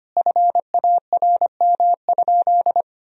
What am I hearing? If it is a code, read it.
Morse code, heard as a keyed tone, about 25 wpm, 700 Hz. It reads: FARM?
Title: Sentences from Top 1000 Words - Review Part 2 - 25wpm